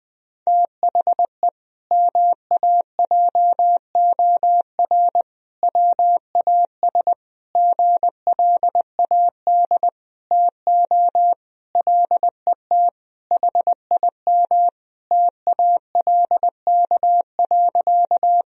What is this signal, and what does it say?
Morse code, heard as a keyed tone, about 20 wpm, 700 Hz. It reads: THE MAJOR WAS GLAD TO LET HIM TALK.
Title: Sentences from Top 1000 Words - Review Part 4 - 20wpm